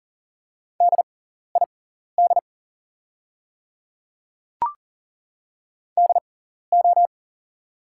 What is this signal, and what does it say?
Morse code, heard as a keyed tone, about 40 wpm, spaced as for 14 wpm, 700 Hz. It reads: DID DO
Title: 6x Character Spacing - Top 100 Words - 40wpm